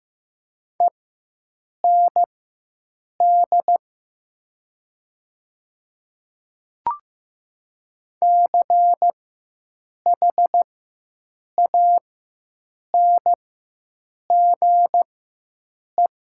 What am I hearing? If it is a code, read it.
Morse code, heard as a keyed tone, about 15 wpm, spaced as for 7 wpm, 700 Hz. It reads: END CHANGE
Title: 4x Character Spacing - Top 200 Words - 15wpm